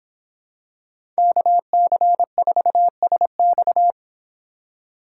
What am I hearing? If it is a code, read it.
Morse code, heard as a keyed tone, about 25 wpm, 700 Hz. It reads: KC4SX